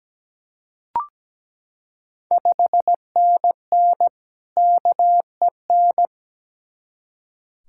Morse code, heard as a keyed tone, about 17 wpm, 700 Hz. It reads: 5NN KEN
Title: Ontario QSO Party Exchange 17wpm